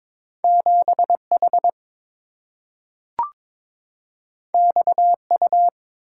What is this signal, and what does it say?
Morse code, heard as a keyed tone, about 22 wpm, 700 Hz. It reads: 7H XU
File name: practice / 22wpm / Two Letter-Number Combinations - 22wpm